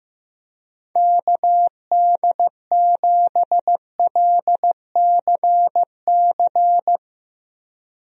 Morse code, heard as a keyed tone, about 16 wpm, 700 Hz. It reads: KD7LCC
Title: USA Call Signs - 6 Characters - Repeated 2x - 15wpm